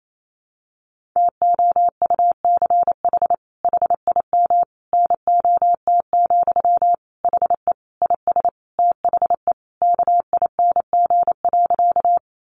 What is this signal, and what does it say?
Morse code, heard as a keyed tone, about 28 wpm, 700 Hz. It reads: TOUCH HIM NOT, HE IS THE KING.
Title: Sentences from Top 1200 Words 28wpm